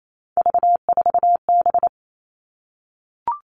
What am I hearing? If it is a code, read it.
Morse code, heard as a keyed tone, about 28 wpm, 700 Hz. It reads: V4B
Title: All Cut Numbers - 3-Digit 28wpm